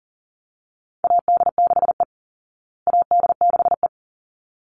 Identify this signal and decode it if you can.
Morse code, heard as a keyed tone, about 40 wpm, 700 Hz. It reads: AD6E AD6E
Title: Call Signs Repeated 3x - 40wpm